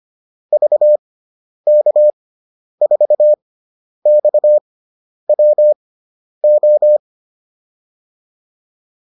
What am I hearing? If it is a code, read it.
Morse code, heard as a keyed tone, about 25 wpm, spaced as for 10 wpm, 600 Hz. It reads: VK4XWO